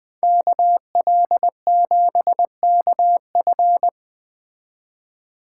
Morse code, heard as a keyed tone, about 20 wpm, 700 Hz. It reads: KL7KF